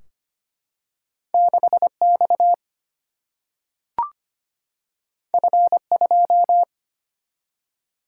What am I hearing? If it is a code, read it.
Morse code, heard as a keyed tone, about 25 wpm, 700 Hz. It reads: 6X F2
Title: Single-Letter Single-Number 25wpm